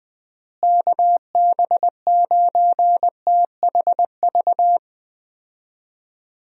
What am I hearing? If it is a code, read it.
Morse code, heard as a keyed tone, about 20 wpm, 700 Hz. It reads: KB9THV
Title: USA Call Signs - 6 Characters - Repeated 2x - 20wpm